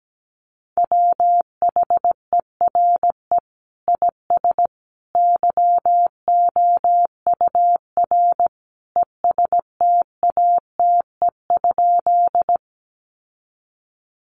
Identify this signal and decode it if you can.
Morse code, heard as a keyed tone, about 17 wpm, 700 Hz. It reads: WHERE IS YOUR ESTATE?
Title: Sentences from Top 1200 Words 17wpm